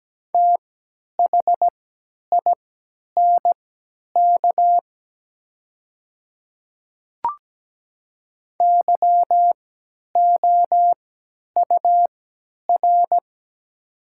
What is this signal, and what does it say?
Morse code, heard as a keyed tone, about 17 wpm, spaced as for 10 wpm, 700 Hz. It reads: THINK YOUR
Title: 3x Character Spacing - Top 200 Words - 17wpm